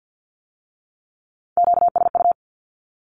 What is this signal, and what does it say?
Morse code, heard as a keyed tone, about 50 wpm, 700 Hz. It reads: QSU
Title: Q-Codes with Spoken Meaning - 50wpm